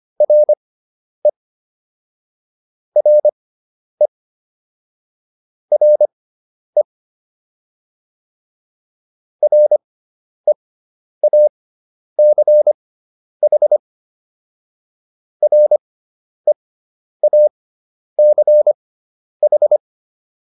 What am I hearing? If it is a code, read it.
Morse code, heard as a keyed tone, about 25 wpm, spaced as for 10 wpm, 600 Hz. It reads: RE RE RE REACH REACH